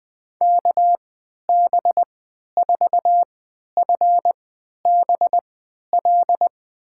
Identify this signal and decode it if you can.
Morse code, heard as a keyed tone, about 20 wpm, spaced as for 11 wpm, 700 Hz. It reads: KB4FBL
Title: USA Call Signs - 6 Characters - 3x Character Spacing - 20wpm